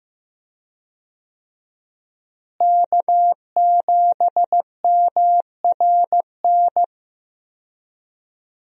Morse code, heard as a keyed tone, about 15 wpm, 700 Hz. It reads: K7MRN
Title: USA Call Signs - 5 Characters - Repeated 3x - 15wpm